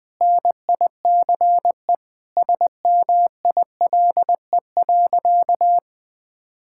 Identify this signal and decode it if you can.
Morse code, heard as a keyed tone, about 20 wpm, 700 Hz. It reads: NICE SMILE.
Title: Sentences with 2-Word Phrases - 20wpm